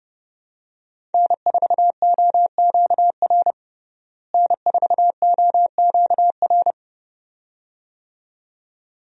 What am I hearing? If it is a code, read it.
Morse code, heard as a keyed tone, about 30 wpm, 700 Hz. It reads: N4OQR N4OQR
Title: USA Call Signs - 5 Characters - Repeated 2x - 30wpm